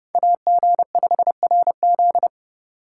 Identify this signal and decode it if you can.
Morse code, heard as a keyed tone, about 29 wpm, 700 Hz. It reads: AG5RZ